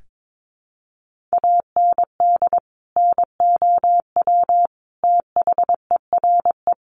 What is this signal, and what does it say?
Morse code, heard as a keyed tone, about 22 wpm, 700 Hz. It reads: AND NOW THERE
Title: Sentences from Top 1400 Words 22wpm